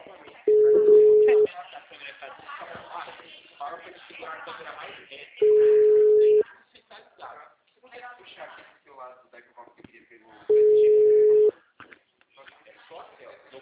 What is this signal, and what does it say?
Ringback tone (Europe, 425 Hz), with speech.